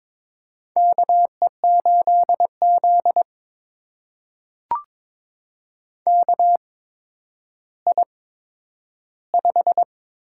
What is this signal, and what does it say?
Morse code, heard as a keyed tone, about 22 wpm, spaced as for 11 wpm, 700 Hz. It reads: KE8Z KI5
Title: USA Call Signs - 8x Character Spacing - 22wpm